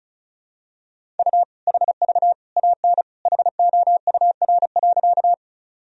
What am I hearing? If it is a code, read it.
Morse code, heard as a keyed tone, about 36 wpm, 700 Hz. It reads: U HV AN HOUR.
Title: Sentences from Top 400 Words and Abbreviations 35wpm